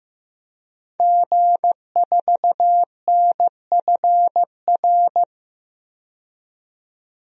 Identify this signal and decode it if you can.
Morse code, heard as a keyed tone, about 15 wpm, 700 Hz. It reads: G4NFR
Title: DXCC-100 Call Signs - 5 Characters - 15wpm